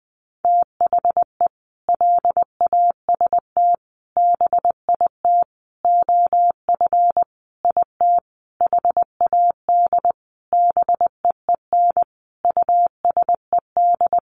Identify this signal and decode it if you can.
Morse code, heard as a keyed tone, about 20 wpm, 700 Hz. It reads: THE LAST BIT OF IT HAD BEEN USED
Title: Sentences from Top 900 Words 20wpm